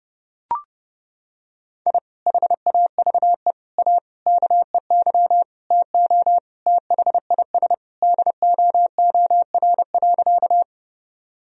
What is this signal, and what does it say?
Morse code, heard as a keyed tone, about 30 wpm, 700 Hz. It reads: I HAVE A KEY TO THIS DOOR.
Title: Sentences from Top 1000 Words - Review Part 4 - 30wpm